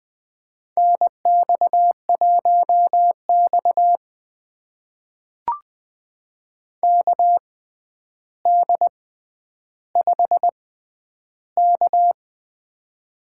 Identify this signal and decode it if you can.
Morse code, heard as a keyed tone, about 20 wpm, spaced as for 10 wpm, 700 Hz. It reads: NX1X KD5K
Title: USA Call Signs - 4 Characters - 6x Character Spacing - 20wpm